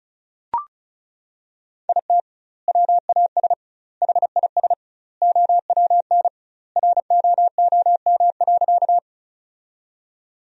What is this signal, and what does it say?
Morse code, heard as a keyed tone, about 36 wpm, 700 Hz. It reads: IT WAS HIS OWN ROOM.
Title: 2x Word-Spacing - Sentences from Top 300 Words - 35wpm